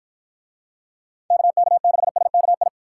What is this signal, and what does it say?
Morse code, heard as a keyed tone, about 53 wpm, 700 Hz. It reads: KK6SKI